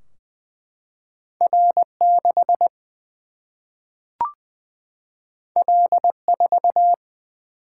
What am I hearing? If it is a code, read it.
Morse code, heard as a keyed tone, about 20 wpm, 700 Hz. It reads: R 6 L 4
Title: Single-Letter Single-Number 20wpm